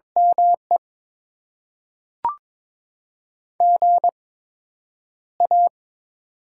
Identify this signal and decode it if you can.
Morse code, heard as a keyed tone, about 21 wpm, spaced as for 9 wpm, 700 Hz. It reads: ME GA
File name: Top 20 QSO Elements - 8x Character Spacing - 22wpm